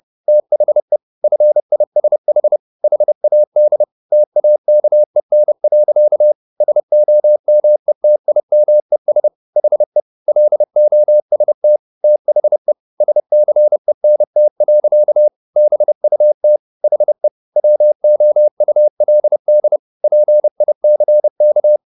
Morse code, heard as a keyed tone, about 30 wpm, 600 Hz. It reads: THE FISH HAD TAKEN. SOMETIMES HE LOST THE SCENT. BUT HE WOULD PICK